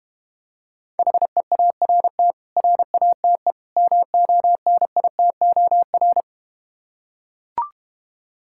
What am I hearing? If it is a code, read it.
Morse code, heard as a keyed tone, about 31 wpm, 700 Hz. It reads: HEART RATE MONITOR
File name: practice / 25wpm / Difficult Words - Speed-Racing with Hints - 25wpm